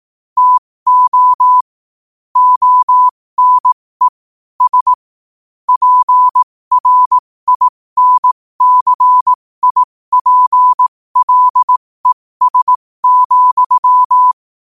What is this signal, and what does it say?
Morse code, heard as a keyed tone, about 18 wpm, spaced as for 15 wpm, 1000 Hz. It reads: TO ONE S PRINCIPLES,
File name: qotdfiles / 15wpm